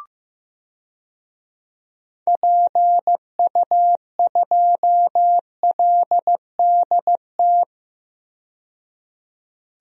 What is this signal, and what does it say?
Morse code, heard as a keyed tone, about 15 wpm, 700 Hz. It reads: PU2LDT
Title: DXCC-100 Call Signs - 6 Characters - 15wpm